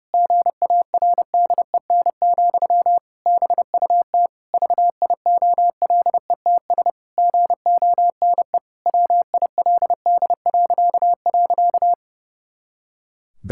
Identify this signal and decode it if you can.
Morse code, heard as a keyed tone, about 30 wpm, 700 Hz. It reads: GARDEN, BUT VIOLETS GONE WILD..
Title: Pangram Challenge v2 - 30wpm